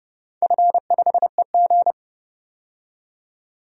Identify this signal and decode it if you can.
Morse code, heard as a keyed tone, about 32 wpm, 700 Hz. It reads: F5EG
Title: DXCC-100 Call Signs - 4 Characters - 30wpm